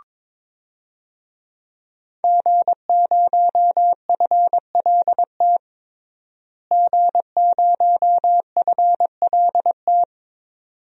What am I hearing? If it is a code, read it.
Morse code, heard as a keyed tone, about 21 wpm, 700 Hz. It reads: G0FLT G0FLT
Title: DXCC-100 Call Signs - 5 Characters - Repeated 2x - 22wpm